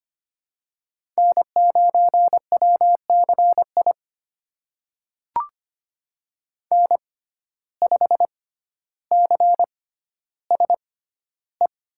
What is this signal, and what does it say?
Morse code, heard as a keyed tone, about 25 wpm, spaced as for 13 wpm, 700 Hz. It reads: N9WCI N5CSE
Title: USA Call Signs - 5 Characters - 6x Character Spacing - 25wpm